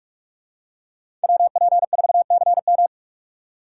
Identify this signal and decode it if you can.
Morse code, heard as a keyed tone, about 46 wpm, 700 Hz. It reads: WP4CM
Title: USA Call Signs - 5 Characters - 2x Character Spacing - 45wpm